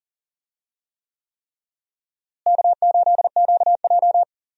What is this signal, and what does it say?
Morse code, heard as a keyed tone, about 39 wpm, 700 Hz. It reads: K8QJ
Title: USA Call Signs - 4 Characters - Repeated 3x - 40wpm